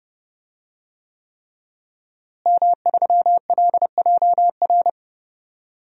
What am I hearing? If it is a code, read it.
Morse code, heard as a keyed tone, about 32 wpm, 700 Hz. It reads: M3LJR
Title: DXCC-100 Call Signs - 5 Characters - Repeated 3x - 30wpm